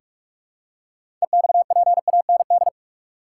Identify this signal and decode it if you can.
Morse code, heard as a keyed tone, about 46 wpm, 700 Hz. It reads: EXPAND
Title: Six Letter Words - 45wpm